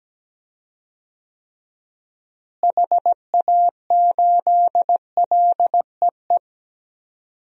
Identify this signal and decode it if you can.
Morse code, heard as a keyed tone, about 17 wpm, 700 Hz. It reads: HA8LEE
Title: DXCC-100 Call Signs - 6 Characters - Repeated 2x - 17wpm